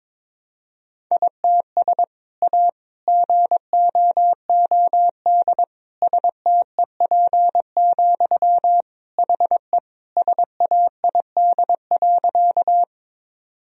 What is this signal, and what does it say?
Morse code, heard as a keyed tone, about 22 wpm, 700 Hz. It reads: ITS A GOOD STEP, HE SAID.
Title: Sentences from Top 1000 Words - Review Part 2 - 22wpm